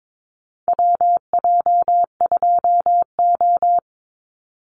Morse code, heard as a keyed tone, about 22 wpm, 700 Hz. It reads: WJ2O
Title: Call Signs - 22wpm